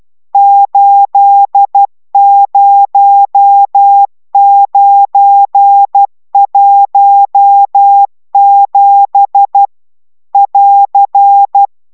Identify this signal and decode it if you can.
Morse code, heard as a keyed tone, about 12 wpm, 800 Hz. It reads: 80917 <AR>